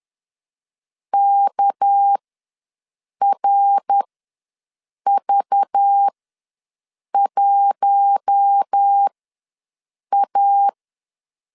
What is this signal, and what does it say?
Morse code, heard as a keyed tone, about 10 wpm, spaced as for 6 wpm, 800 Hz. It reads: KRV1A
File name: LEZIONI / 6 PKX0 / 6 PK ws 10wpm 3745